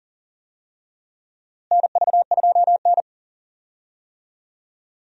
Morse code, heard as a keyed tone, about 40 wpm, 700 Hz. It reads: NV2N
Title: USA Call Signs - 4 Characters - Repeated 3x - 40wpm